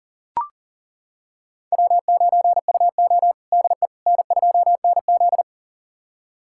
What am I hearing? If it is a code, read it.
Morse code, heard as a keyed tone, about 40 wpm, 700 Hz. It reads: W9UO DE N2NZ
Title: USA Callsign DE Callsign - 4 Characters - 40wpm